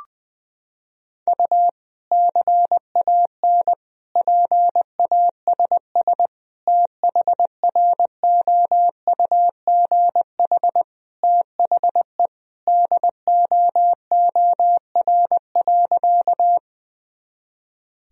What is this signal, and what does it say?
Morse code, heard as a keyed tone, about 20 wpm, 700 Hz. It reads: U CAN PASS THROUGH THE DOOR.
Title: Sentences from Top 400 Words and Abbreviations 20wpm